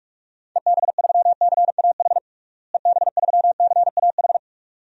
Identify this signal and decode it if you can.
Morse code, heard as a keyed tone, about 46 wpm, 700 Hz. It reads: EB3CAH EB3CAH
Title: DXCC-100 Call Signs - 6 Characters - Repeated 2x - 45wpm